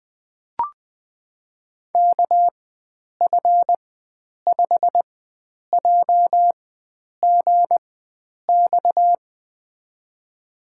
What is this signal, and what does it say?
Morse code, heard as a keyed tone, about 21 wpm, spaced as for 9 wpm, 700 Hz. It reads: KF5JGX
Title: USA Call Signs - 6 Characters - 4x Character Spacing - 20wpm